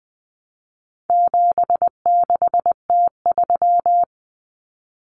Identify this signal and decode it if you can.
Morse code, heard as a keyed tone, about 20 wpm, 700 Hz. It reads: 76T3
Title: Common Cut Numbers - 4-Digit 20wpm